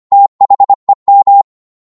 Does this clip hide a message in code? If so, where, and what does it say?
Yes: Morse code from 0.1 to 1.4 s, heard as a keyed tone, about 25 wpm, 800 Hz. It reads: THEM